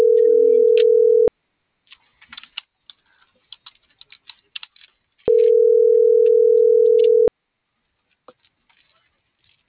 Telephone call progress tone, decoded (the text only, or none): ringback tone